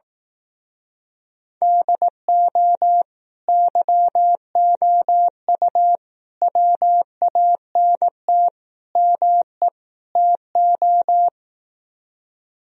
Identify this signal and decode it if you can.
Morse code, heard as a keyed tone, about 18 wpm, 700 Hz. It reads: DO YOU WANT ME TO
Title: Speed-Racing - Sets of 5 Words 15wpm